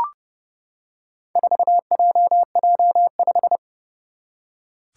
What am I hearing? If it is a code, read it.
Morse code, heard as a keyed tone, about 30 wpm, 700 Hz. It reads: 4JJ5
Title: Four Challenging Letter-Number Combinations 30wpm